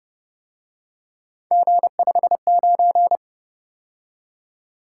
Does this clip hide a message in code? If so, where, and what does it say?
Yes: Morse code from 1.5 to 3.2 s, heard as a keyed tone, about 28 wpm, 700 Hz. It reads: G59